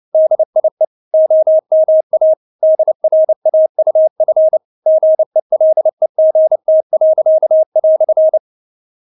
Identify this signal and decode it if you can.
Morse code, heard as a keyed tone, about 29 wpm, 630 Hz. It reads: DIE OMA DRAUF GELEGT."
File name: Witz10 29wpm